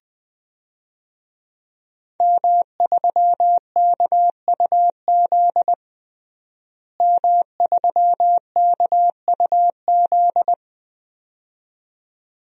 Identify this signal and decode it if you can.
Morse code, heard as a keyed tone, about 20 wpm, 700 Hz. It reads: M3KUZ M3KUZ
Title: DXCC-100 Call Signs - 5 Characters - Repeated 2x - 20wpm